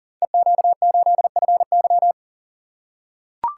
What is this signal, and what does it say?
Morse code, heard as a keyed tone, about 38 wpm, 690 Hz. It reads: EQ8FY
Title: Code Groups - 5 Letter:Numbers - 40wpm